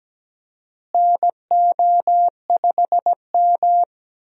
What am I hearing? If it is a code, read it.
Morse code, heard as a keyed tone, about 17 wpm, 700 Hz. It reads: NO5M
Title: USA Call Signs - 4 Characters - Repeated 3x - 17wpm